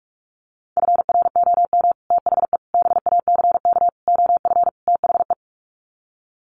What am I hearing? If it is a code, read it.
Morse code, heard as a keyed tone, about 44 wpm, 700 Hz. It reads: FROM THE BACK OF THE
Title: Sets of 5 Words Encore 45wpm